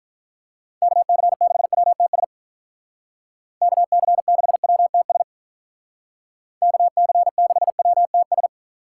Morse code, heard as a keyed tone, about 45 wpm, 700 Hz. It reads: KC6WTS KC6WTS KC6WTS